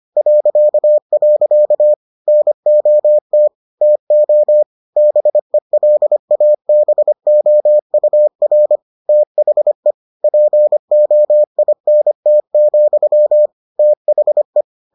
Morse code, heard as a keyed tone, about 25 wpm, 600 Hz. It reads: .. NOT TO BELABOUR THE POINT, THE